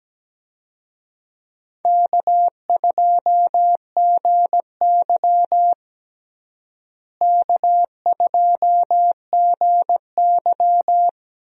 Morse code, heard as a keyed tone, about 17 wpm, 700 Hz. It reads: K2GY K2GY
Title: USA Call Signs - 4 Characters - Repeated 2x - 17wpm